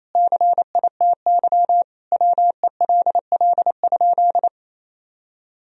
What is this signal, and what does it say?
Morse code, heard as a keyed tone, about 28 wpm, 700 Hz. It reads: CITY WELL?
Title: Sentences from Top 1000 Words - Review Part 2 - 28wpm